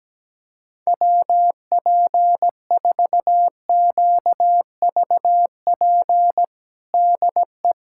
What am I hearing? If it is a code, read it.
Morse code, heard as a keyed tone, about 17 wpm, 700 Hz. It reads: WP4QVP DE
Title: USA Callsign DE Callsign 17wpm